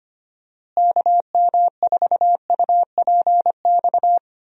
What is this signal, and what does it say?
Morse code, heard as a keyed tone, about 25 wpm, 700 Hz. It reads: KM4UPX
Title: USA Call Signs - 6 Characters - Repeated 2x - 25wpm